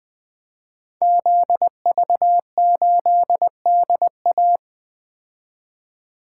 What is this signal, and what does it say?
Morse code, heard as a keyed tone, about 20 wpm, 700 Hz. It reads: ZV8DA
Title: Code Groups - 5 Letter:Numbers - 20wpm